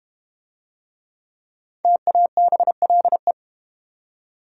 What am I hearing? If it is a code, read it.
Morse code, heard as a keyed tone, about 30 wpm, 700 Hz. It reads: TABLE
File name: Easy Words - Speed-Racing with Hints - 25wpm